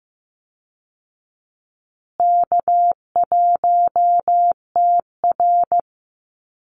Morse code, heard as a keyed tone, about 15 wpm, 700 Hz. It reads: K1TR
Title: Easy Call Signs Repeated 3x - 15wpm